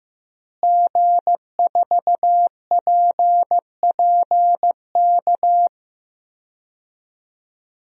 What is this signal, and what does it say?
Morse code, heard as a keyed tone, about 15 wpm, 700 Hz. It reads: G4PPK
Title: DXCC-100 Call Signs - 5 Characters - Repeated 2x - 15wpm